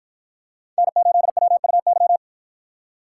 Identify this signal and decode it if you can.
Morse code, heard as a keyed tone, about 52 wpm, 700 Hz. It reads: N8WUY